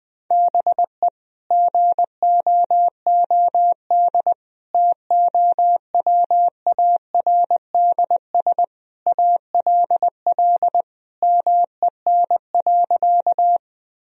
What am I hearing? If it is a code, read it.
Morse code, heard as a keyed tone, about 19 wpm, 700 Hz. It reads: BE GOOD TOWARDS ALL MEN.
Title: Sentences from Top 1000 Words - Review Part 2 - 20wpm